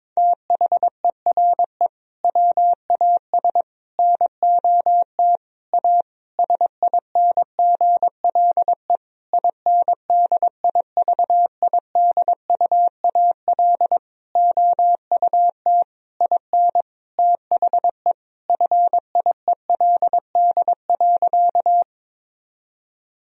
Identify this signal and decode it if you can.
Morse code, heard as a keyed tone, about 22 wpm, 700 Hz. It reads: THERE WAS NOT A SINGLE INDIVIDUAL OUT IN THE FIELD.
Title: Sentences from Top 1000 Words - Review Part 4 - 22wpm